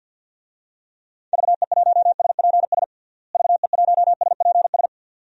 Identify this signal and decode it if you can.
Morse code, heard as a keyed tone, about 52 wpm, 700 Hz. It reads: VE1SPS VE1SPS
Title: DXCC-100 Call Signs - 6 Characters - Repeated 2x - 50wpm